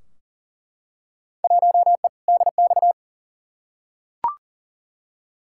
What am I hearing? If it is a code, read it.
Morse code, heard as a keyed tone, about 40 wpm, 700 Hz. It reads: 1E DX
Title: Field Day Exchanges Encore - 40wpm